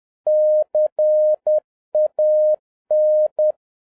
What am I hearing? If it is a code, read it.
Morse code, heard as a keyed tone, about 10 wpm, 610 Hz. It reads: CAN
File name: 2011-07-mp3 / n21 GREETINGS FROM SINGAPORE 10wpm